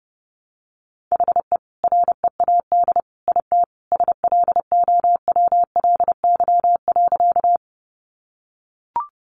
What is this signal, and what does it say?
Morse code, heard as a keyed tone, about 30 wpm, 700 Hz. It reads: HE READ IT SLOWLY.
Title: Sentences from Top 1100 Words 30wpm